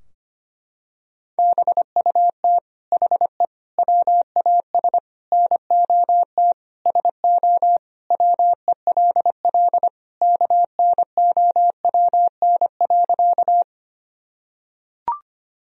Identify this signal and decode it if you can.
Morse code, heard as a keyed tone, about 25 wpm, 700 Hz. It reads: BUT HE WAS NOT SO WELL KNOWN.
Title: Sentences from Top 1000 Words - Review Part 3 - 25wpm